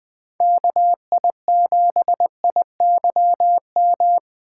Morse code, heard as a keyed tone, about 20 wpm, 700 Hz. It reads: KI7IYM
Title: Speed Racing - USA Call Signs 17wpm